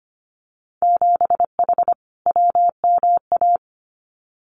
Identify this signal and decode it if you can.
Morse code, heard as a keyed tone, about 25 wpm, 700 Hz. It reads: 7H WMA
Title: Winter Field Day Exchanges Encore 25wpm